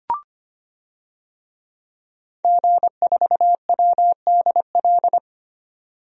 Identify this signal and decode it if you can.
Morse code, heard as a keyed tone, about 25 wpm, 700 Hz. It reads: G4WDL
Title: DXCC-100 Call Signs - 5 Characters - 25wpm